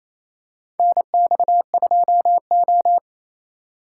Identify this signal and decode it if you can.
Morse code, heard as a keyed tone, about 28 wpm, 700 Hz. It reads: NX2O
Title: USA Call Signs - 4 Characters - 2x Character Spacing - 28wpm